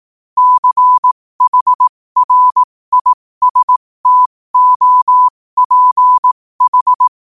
Morse code, heard as a keyed tone, about 18 wpm, spaced as for 15 wpm, 1000 Hz. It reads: CHRISTOPH
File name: qotdfiles / 15wpm